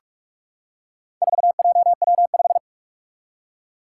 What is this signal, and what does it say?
Morse code, heard as a keyed tone, about 46 wpm, 700 Hz. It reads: 4JW5
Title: Four Challenging Letter-Number Combinations 45wpm